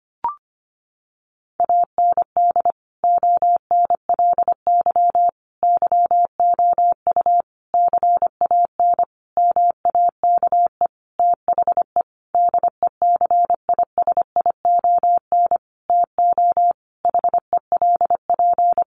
Morse code, heard as a keyed tone, about 25 wpm, 700 Hz. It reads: AND ONLY YOU CAN MAKE THE DECISION TO HELP
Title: Sentences from Top 1400 Words 25wpm